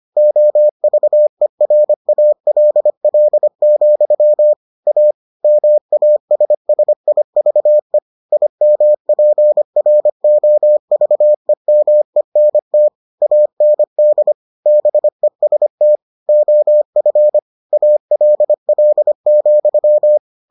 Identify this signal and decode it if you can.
Morse code, heard as a keyed tone, about 25 wpm, 600 Hz. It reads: OVERALL, A MASSIVE IMPROVEMENT AND BEST OF ALL,